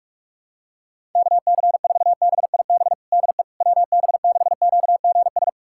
Morse code, heard as a keyed tone, about 43 wpm, 700 Hz. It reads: KC4BIB DE WB6QGS
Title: USA Callsign DE Callsign - 6 Characters - 45wpm